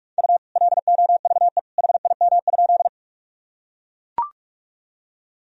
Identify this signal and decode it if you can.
Morse code, heard as a keyed tone, about 41 wpm, 700 Hz. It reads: U LOVE HIM?
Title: Sentences from Top 400 Words and Abbreviations 45wpm